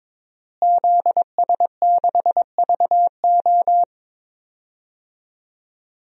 Morse code, heard as a keyed tone, about 22 wpm, 700 Hz. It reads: ZS6VO
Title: DXCC-100 Call Signs - 5 Characters - Repeated 2x - 22wpm